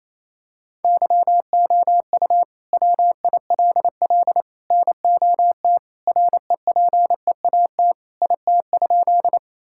Morse code, heard as a keyed tone, about 28 wpm, 700 Hz. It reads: YOU WILL NOT REPEAT IT?
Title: Sentences from Top 1600 Words 28wpm